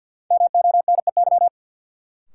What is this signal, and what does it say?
Morse code, heard as a keyed tone, about 52 wpm, 700 Hz. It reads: MONEY
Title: Sets of 4 Words - New Word Drill - 50wpm